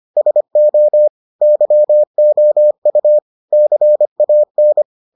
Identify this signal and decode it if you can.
Morse code, heard as a keyed tone, about 25 wpm, 600 Hz. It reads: SO YOU CAN